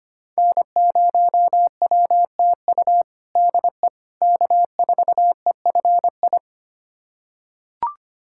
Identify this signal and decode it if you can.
Morse code, heard as a keyed tone, about 25 wpm, 700 Hz. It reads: N0WTU DE K4EFI
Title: USA Callsign DE Callsign - 5 Characters - 25wpm